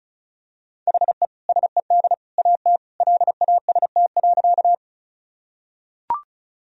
Morse code, heard as a keyed tone, about 36 wpm, 700 Hz. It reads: HE SED AT LAST.
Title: Sentences from Top 300 Words and Abbreviations 35wpm